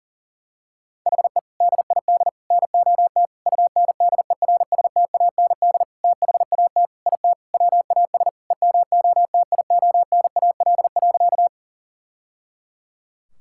Morse code, heard as a keyed tone, about 40 wpm, 700 Hz. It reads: HE DID NOT UNDERSTAND THAT IT WAS EMOTIONAL.
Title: Sentences from Top 1000 Words - Review Part 5 - 40wpm